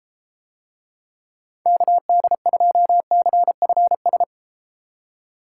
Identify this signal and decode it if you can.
Morse code, heard as a keyed tone, about 32 wpm, 700 Hz. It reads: KD2CFS